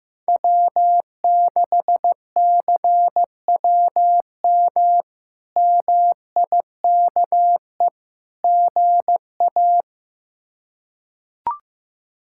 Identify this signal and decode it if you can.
Morse code, heard as a keyed tone, about 16 wpm, 700 Hz. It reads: W6CWM MIKE GA
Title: K1USN SST QSO Practice - 15wpm